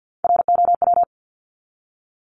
Morse code, heard as a keyed tone, about 50 wpm, 700 Hz. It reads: ROW
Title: Top 1751-2000 Words 50wpm